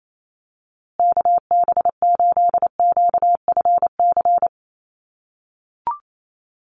Morse code, heard as a keyed tone, about 28 wpm, 700 Hz. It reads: KB8QFC
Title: USA Call Signs Repeated 2x 28wpm